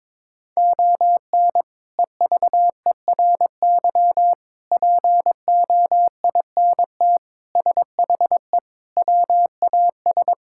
Morse code, heard as a keyed tone, about 22 wpm, 700 Hz. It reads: ON EVERY POINT SHE WAS
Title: Sentences from Top 1000 Words - Review Part 6 - 22wpm